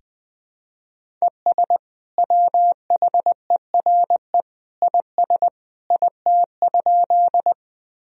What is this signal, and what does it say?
Morse code, heard as a keyed tone, about 20 wpm, 700 Hz. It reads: ES WHERE IS IT?
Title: Sentences from Top 200 Words and Abbreviations 20wpm